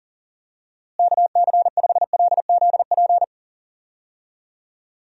Morse code, heard as a keyed tone, about 40 wpm, 700 Hz. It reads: KC5LZP
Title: USA Call Signs - 6 Characters - Repeated 2x - 40wpm